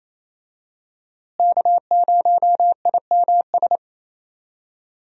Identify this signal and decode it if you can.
Morse code, heard as a keyed tone, about 27 wpm, 700 Hz. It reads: K0IMS
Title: USA Call Signs - 5 Characters - Repeated 2x - 28wpm